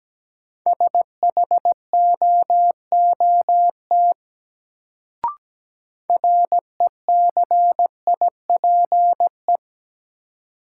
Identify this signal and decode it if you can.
Morse code, heard as a keyed tone, about 17 wpm, 700 Hz. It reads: SHOOT RECIPE
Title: Sentences from Top 1200 Words - New Word Drill - 17wpm